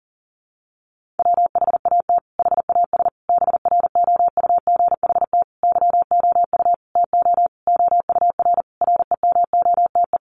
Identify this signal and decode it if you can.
Morse code, heard as a keyed tone, about 40 wpm, 700 Hz. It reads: WHAT HAS BROUGHT YOU TO OUR REMOTE
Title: Sentences from Top 1500 Words 40wpm